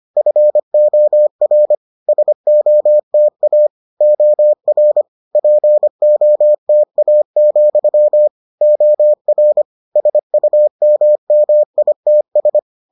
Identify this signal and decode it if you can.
Morse code, heard as a keyed tone, about 25 wpm, 600 Hz. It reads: FOR SOTA OR POTA, OR SUMMITS